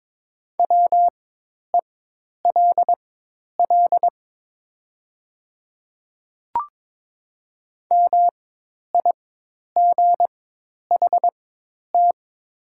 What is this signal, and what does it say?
Morse code, heard as a keyed tone, about 22 wpm, spaced as for 10 wpm, 700 Hz. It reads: WELL MIGHT
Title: 4x Character Spacing - Top 300 Words - 22wpm